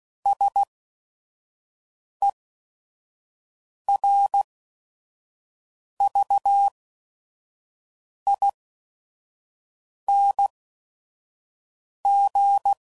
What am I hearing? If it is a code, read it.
Morse code, heard as a keyed tone, about 16 wpm, spaced as for 5 wpm, 800 Hz. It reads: SERVING
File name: Suffixes / Special ING's / VerySlowToFaster ING's